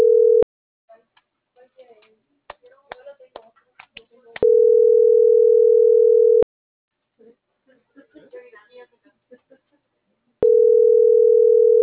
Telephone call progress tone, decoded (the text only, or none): ringback tone